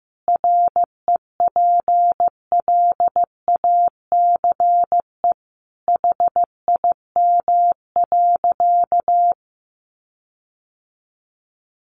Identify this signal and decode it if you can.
Morse code, heard as a keyed tone, about 15 wpm, 700 Hz. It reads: REPLACE HIM.
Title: Sentences from Top 1400 Words 15wpm